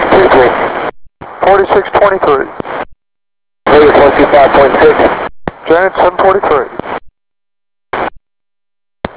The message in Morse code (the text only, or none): none